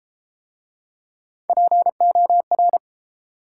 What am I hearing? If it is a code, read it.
Morse code, heard as a keyed tone, about 31 wpm, 700 Hz. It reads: POR